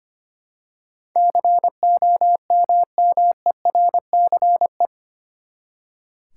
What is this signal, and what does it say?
Morse code, heard as a keyed tone, about 25 wpm, 700 Hz. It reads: COMMERCE
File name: Eight Letter Words - 25wpm